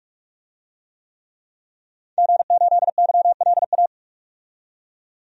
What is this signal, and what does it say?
Morse code, heard as a keyed tone, about 46 wpm, 700 Hz. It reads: G8YLA
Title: DXCC-100 Call Signs - 5 Characters - 45wpm